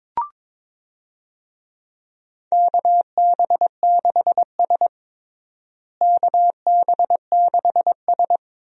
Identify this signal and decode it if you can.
Morse code, heard as a keyed tone, about 22 wpm, 700 Hz. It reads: KB6S KB6S
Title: USA Call Signs - 4 Characters - Repeated 3x - 22wpm